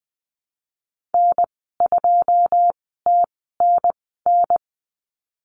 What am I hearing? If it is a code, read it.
Morse code, heard as a keyed tone, about 20 wpm, spaced as for 15 wpm, 700 Hz. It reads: N2TNN
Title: 161 Lesson 26g - Letter g - Call Signs 20wpm